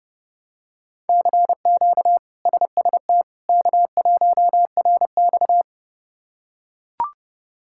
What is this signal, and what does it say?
Morse code, heard as a keyed tone, about 30 wpm, 700 Hz. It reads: CQ SST K1RX
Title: K1USN SST QSO Practice - 30wpm